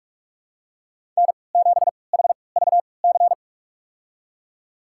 Morse code, heard as a keyed tone, about 43 wpm, spaced as for 25 wpm, 700 Hz. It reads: N7HVC
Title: USA Call Signs - 5 Characters - 3x Character Spacing - 45wpm